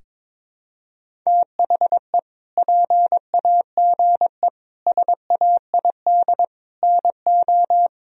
Morse code, heard as a keyed tone, about 22 wpm, 700 Hz. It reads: THE PAGE SAID NO
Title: Sentences from Top 1000 Words - Review Part 1 - 22wpm